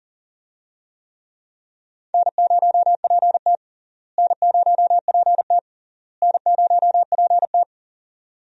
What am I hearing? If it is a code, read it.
Morse code, heard as a keyed tone, about 40 wpm, 700 Hz. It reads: N0PT N0PT N0PT
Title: USA Call Signs - 4 Characters - Repeated 3x - 40wpm